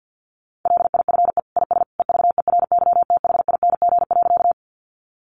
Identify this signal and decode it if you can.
Morse code, heard as a keyed tone, about 50 wpm, 700 Hz. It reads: LIFE IS EVERYTHING.
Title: Sentences from Top 700 Words 50wpm